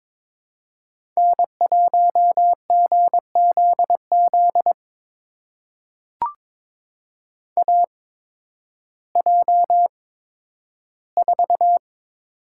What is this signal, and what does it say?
Morse code, heard as a keyed tone, about 21 wpm, spaced as for 12 wpm, 700 Hz. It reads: N1GZZ AJ4
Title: USA Call Signs - 8x Character Spacing - 22wpm